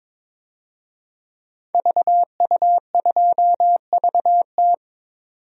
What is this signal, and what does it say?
Morse code, heard as a keyed tone, about 22 wpm, 700 Hz. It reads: VU2VT
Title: DXCC-100 Call Signs - 5 Characters - 22wpm